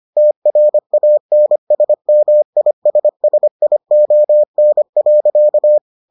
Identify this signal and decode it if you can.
Morse code, heard as a keyed tone, about 25 wpm, 600 Hz. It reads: TRANSMISSION.